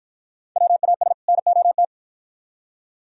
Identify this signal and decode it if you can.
Morse code, heard as a keyed tone, about 49 wpm, 700 Hz. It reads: WAS NOT